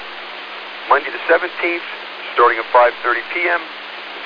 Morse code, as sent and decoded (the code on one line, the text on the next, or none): none